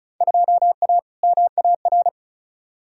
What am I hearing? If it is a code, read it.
Morse code, heard as a keyed tone, about 34 wpm, 700 Hz. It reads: 2A MAR
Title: Field Day Exchanges - 35wpm